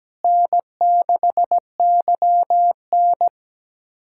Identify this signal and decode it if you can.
Morse code, heard as a keyed tone, about 17 wpm, 700 Hz. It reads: N6YN
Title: USA Callsign DE Callsign - 4 Characters - 17wpm